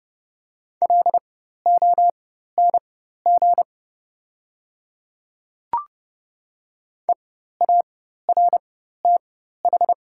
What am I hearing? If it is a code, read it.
Morse code, heard as a keyed tone, about 30 wpm, spaced as for 14 wpm, 700 Hz. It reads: LONG EARTH
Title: 4x Character Spacing - Top 200 Words - 30wpm